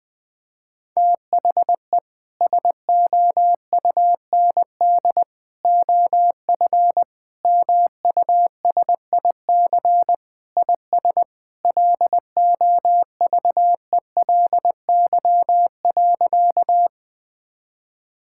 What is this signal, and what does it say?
Morse code, heard as a keyed tone, about 20 wpm, 700 Hz. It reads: THE SOUND OF MUSIC IS LOVELY.
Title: Sentences with 3-Word Phrases - Spoken Beginning - 20wpm